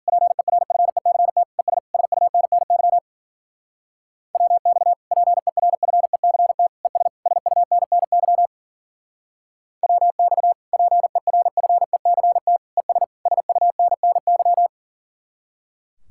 Morse code, heard as a keyed tone, about 52 wpm, 700 Hz. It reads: PERFECT ES SUNNY WX PERFECT ES SUNNY WX PERFECT ES SUNNY